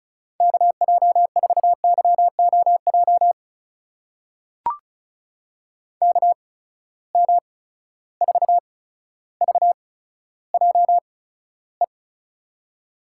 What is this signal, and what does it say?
Morse code, heard as a keyed tone, about 36 wpm, spaced as for 15 wpm, 700 Hz. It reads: KJ4YOJ KM4VJE
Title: USA Call Signs - 6 Characters - 8x Character Spacing - 35wpm